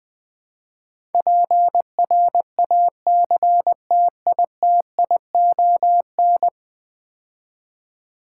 Morse code, heard as a keyed tone, about 20 wpm, 700 Hz. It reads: PRACTITION
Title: Build A Word - 5 Letters At A Time - 12-Letter Words - 20wpm